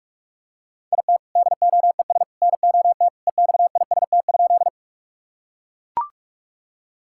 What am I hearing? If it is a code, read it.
Morse code, heard as a keyed tone, about 41 wpm, 700 Hz. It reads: IT DOES NOT EXIST?
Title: Sentences from Top 1000 Words - Review Part 5 - 45wpm